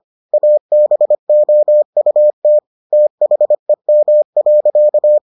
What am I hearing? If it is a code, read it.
Morse code, heard as a keyed tone, about 25 wpm, 600 Hz. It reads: ABOUT THEM.